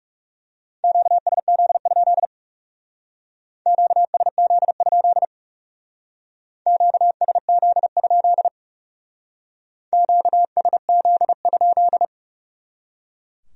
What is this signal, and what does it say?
Morse code, heard as a keyed tone, about 38 wpm, 700 Hz. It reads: QSZ? QSZ? QSZ? QSZ?